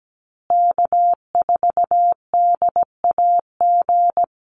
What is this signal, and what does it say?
Morse code, heard as a keyed tone, about 17 wpm, 700 Hz. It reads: K4DAG
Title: USA Call Signs Repeated 3x 17wpm